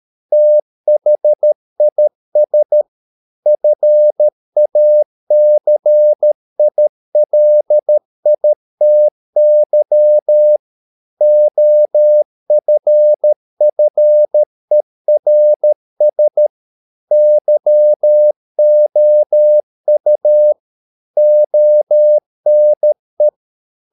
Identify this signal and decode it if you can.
Morse code, heard as a keyed tone, about 13 wpm, 600 Hz. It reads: THIS FACILITY OFFERS YOU ONE